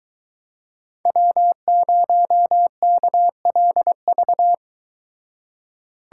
Morse code, heard as a keyed tone, about 22 wpm, 700 Hz. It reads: W0KLV